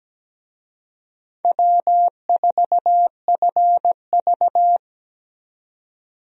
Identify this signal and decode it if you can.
Morse code, heard as a keyed tone, about 17 wpm, 700 Hz. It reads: W4FV
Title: USA Call Signs - 4 Characters - Repeated 2x - 17wpm